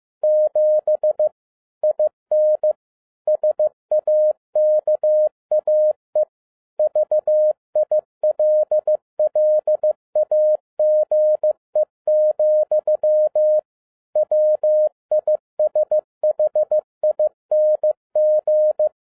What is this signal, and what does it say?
Morse code, heard as a keyed tone, about 15 wpm, 610 Hz. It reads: 7 IN SAKAE VILLAGE, WISHING
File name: FIELD DAY CONTEST QRV FROM SAKAE VILLAGE 15wpm